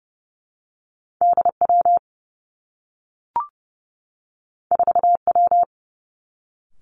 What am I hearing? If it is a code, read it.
Morse code, heard as a keyed tone, about 30 wpm, 700 Hz. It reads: D W 4 W
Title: Two Challenging Letter-Number Combinations 30wpm